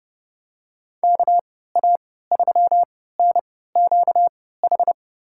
Morse code, heard as a keyed tone, about 30 wpm, spaced as for 17 wpm, 700 Hz. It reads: KA3NQH
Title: USA Call Signs - 6 Characters - 3x Character Spacing - 30wpm